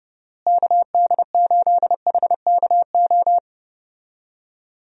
Morse code, heard as a keyed tone, about 30 wpm, 700 Hz. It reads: KD8HKO
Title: USA Call Signs - 6 Characters - Repeated 3x - 30wpm